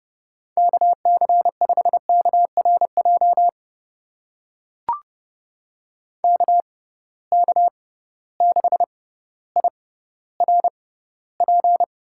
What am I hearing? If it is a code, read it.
Morse code, heard as a keyed tone, about 31 wpm, spaced as for 16 wpm, 700 Hz. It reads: KC5KRJ KK6IRP
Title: USA Call Signs - 6 Characters - 6x Character Spacing - 30wpm